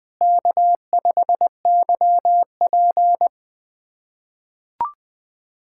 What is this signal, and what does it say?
Morse code, heard as a keyed tone, about 20 wpm, 700 Hz. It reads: K5YP
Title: USA Callsign DE Callsign - 4 Characters - 20wpm